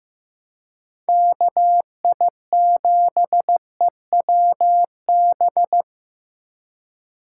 Morse code, heard as a keyed tone, about 18 wpm, 700 Hz. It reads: KI7EWB
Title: Speed Racing - USA 2x3 Call Signs 15wpm